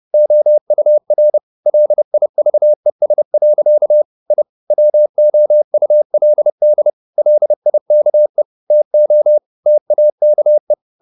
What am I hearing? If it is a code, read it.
Morse code, heard as a keyed tone, about 30 wpm, 600 Hz. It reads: OUR LIVES. I WOULD LIKE TO TAKE
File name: the old man and the sea / the old man and the sea 1 30wpm